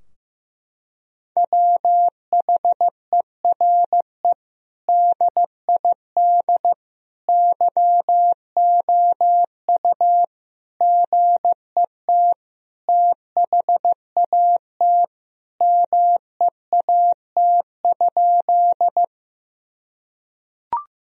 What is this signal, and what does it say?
Morse code, heard as a keyed tone, about 16 wpm, 700 Hz. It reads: WHERE DID YOU GET THAT MEAT?
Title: Sentences from Top 1000 Words - Review Part 3 - 15wpm